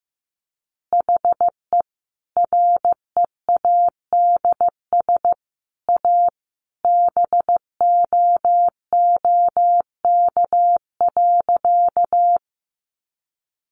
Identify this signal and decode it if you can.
Morse code, heard as a keyed tone, about 15 wpm, 700 Hz. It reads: HE READS A BOOK.
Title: Additional Easy Phrases 15wpm